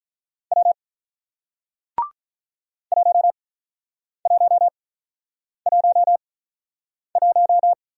Morse code, heard as a keyed tone, about 48 wpm, spaced as for 9 wpm, 700 Hz. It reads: W 1111